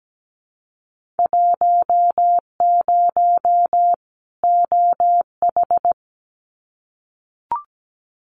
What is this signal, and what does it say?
Morse code, heard as a keyed tone, about 17 wpm, 700 Hz. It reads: 10 OH
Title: Winter Field Day Exchanges 17wpm